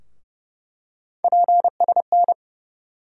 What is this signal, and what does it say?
Morse code, heard as a keyed tone, about 30 wpm, 700 Hz. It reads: PSN
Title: USA Call Sign Postfixes - 3 Letters 30wpm